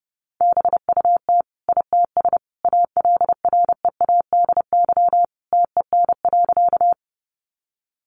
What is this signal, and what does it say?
Morse code, heard as a keyed tone, about 30 wpm, 700 Hz. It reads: BUT ITS ALREADY TEN.
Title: Sentences from Top 700 Words 30wpm